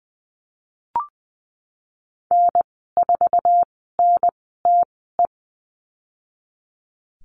Morse code, heard as a keyed tone, about 20 wpm, spaced as for 14 wpm, 700 Hz. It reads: N4NTE
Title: 088 Lesson 15g - Number 5 - Call Signs 20wpm